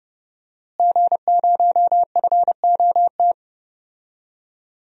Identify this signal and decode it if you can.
Morse code, heard as a keyed tone, about 29 wpm, 700 Hz. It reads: G0FOT
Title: DXCC-100 Call Signs - 5 Characters - Repeated 3x - 30wpm